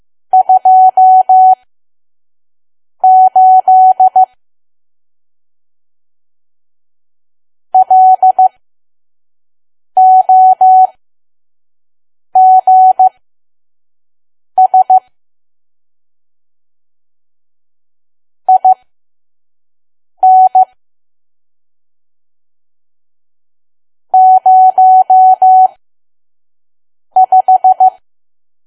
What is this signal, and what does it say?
Morse code, heard as a keyed tone, about 15 wpm, spaced as for 5 wpm, 750 Hz. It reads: 28 LOGS IN 05
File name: Archive / 5 WPM files / PM